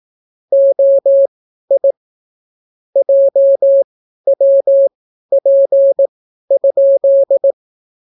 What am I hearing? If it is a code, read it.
Morse code, heard as a keyed tone, about 18 wpm, spaced as for 12 wpm, 550 Hz. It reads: OI JWP?